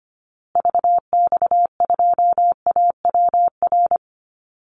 Morse code, heard as a keyed tone, about 25 wpm, 700 Hz. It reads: VX2AWR
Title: Call Signs Repeated 2x - 25wpm